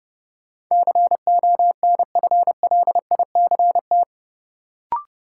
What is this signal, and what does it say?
Morse code, heard as a keyed tone, about 30 wpm, 700 Hz. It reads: CONFLICT
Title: Sentences from Top 1300 Words - New Word Drill - 30wpm